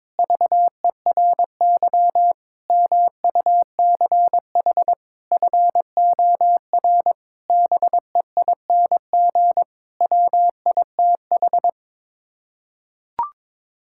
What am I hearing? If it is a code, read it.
Morse code, heard as a keyed tone, about 22 wpm, 700 Hz. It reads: VERY MUCH FOR BEING WITH
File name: Sets of 9 Words - 22wpm